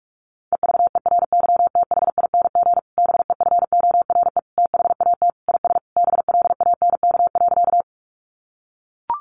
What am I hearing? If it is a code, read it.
Morse code, heard as a keyed tone, about 44 wpm, 700 Hz. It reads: EVERYTHING BEFORE THAT IS BLANK.
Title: Sentences from Top 1500 Words 45wpm